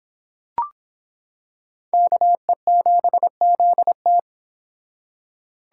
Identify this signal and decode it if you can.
Morse code, heard as a keyed tone, about 25 wpm, 700 Hz. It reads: KE7ZT